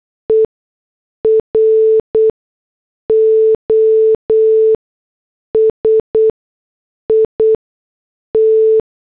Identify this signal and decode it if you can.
Morse code, heard as a keyed tone, about 8 wpm, spaced as for 6 wpm, 440 Hz. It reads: EROSIT